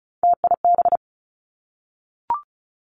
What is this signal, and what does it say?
Morse code, heard as a keyed tone, about 36 wpm, 700 Hz. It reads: TIB
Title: Three Challenging Letter-Number Combinations 35wpm